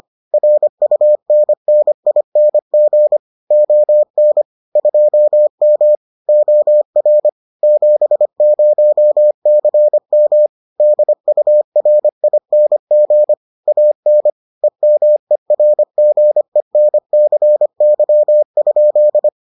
Morse code, heard as a keyed tone, about 25 wpm, 600 Hz. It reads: RUNNING ON 2M OR 70CM DURING AN EMERGENCY?